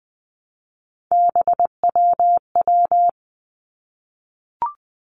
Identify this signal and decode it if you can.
Morse code, heard as a keyed tone, about 20 wpm, 700 Hz. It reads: BWW
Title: Three Challenging Letter-Number Combinations 20wpm